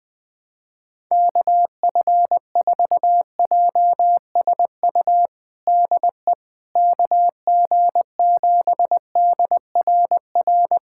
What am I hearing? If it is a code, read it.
Morse code, heard as a keyed tone, about 20 wpm, 700 Hz. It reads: KF4JSU DE KG7DRR
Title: USA Callsign DE Callsign 20wpm